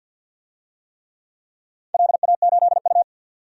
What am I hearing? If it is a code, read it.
Morse code, heard as a keyed tone, about 49 wpm, 700 Hz. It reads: LA8U
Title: DXCC-100 Call Signs - 4 Characters - 50wpm